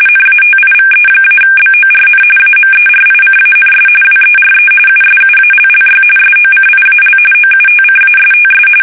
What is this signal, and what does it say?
RTTY, about 45.45 baud. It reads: NATIVE-BORN AMERICANS WERE USUALLY SOCIAL OUTCASTS
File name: fsk850 / usnavy2